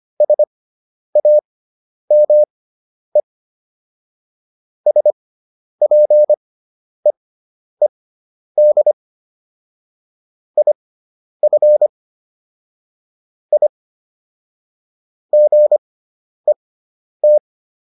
Morse code, heard as a keyed tone, about 25 wpm, spaced as for 10 wpm, 600 Hz. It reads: SAME SPEED IF I GET